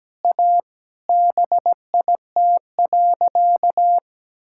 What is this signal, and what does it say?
Morse code, heard as a keyed tone, about 17 wpm, 700 Hz. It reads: A BIT.
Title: Sentences from Top 1000 Words - Review Part 5 - 17wpm